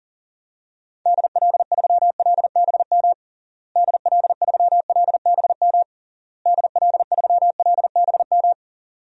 Morse code, heard as a keyed tone, about 40 wpm, 700 Hz. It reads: DL3LBM DL3LBM DL3LBM
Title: DXCC-100 Call Signs - 6 Characters - Repeated 3x - 40wpm